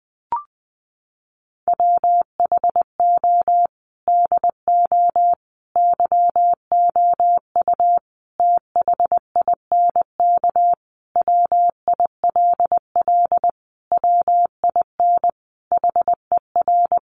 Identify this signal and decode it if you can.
Morse code, heard as a keyed tone, about 20 wpm, 700 Hz. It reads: WHO DO YOU THINK WILL WIN HER
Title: Sentences from Top 1500 Words 20wpm